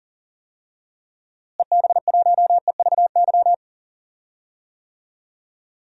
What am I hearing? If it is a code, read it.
Morse code, heard as a keyed tone, about 40 wpm, 700 Hz. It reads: EB1EVY
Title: DXCC-100 Call Signs - 6 Characters - Repeated 3x - 40wpm